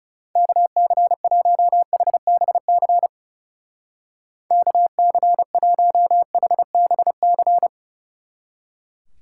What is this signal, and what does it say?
Morse code, heard as a keyed tone, about 34 wpm, 700 Hz. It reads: KC1HBC KC1HBC
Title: Speed Racing - USA 2x3 Call Signs 30wpm